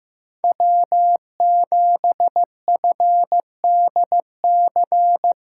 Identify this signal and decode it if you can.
Morse code, heard as a keyed tone, about 16 wpm, 700 Hz. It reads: W7FDC
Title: USA Callsign DE Callsign - 5 Characters - 15wpm